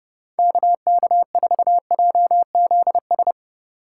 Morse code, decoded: KK4JZS

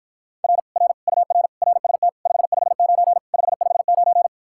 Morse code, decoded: R R UR RST 559 559

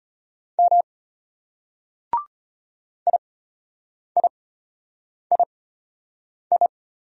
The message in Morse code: M IIII

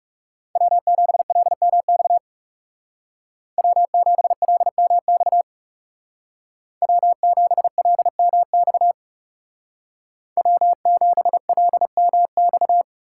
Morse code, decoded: W7LMX W7LMX W7LMX W7LMX

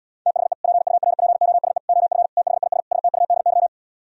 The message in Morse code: THE PURPOSE OF THIS STUDY

